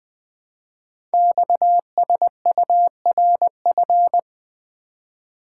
Morse code, heard as a keyed tone, about 20 wpm, 700 Hz. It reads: XSURF
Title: Code Groups - 5 Letters - 20wpm